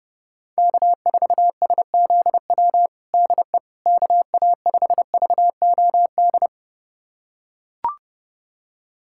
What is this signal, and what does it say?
Morse code, heard as a keyed tone, about 32 wpm, 700 Hz. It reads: K4SZW DE KA5VOD
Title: USA Callsign DE Callsign 30wpm